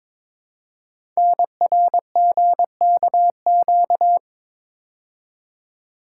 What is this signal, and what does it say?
Morse code, heard as a keyed tone, about 22 wpm, 700 Hz. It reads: NRGKQ
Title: Code Groups - 5 Letters - 22wpm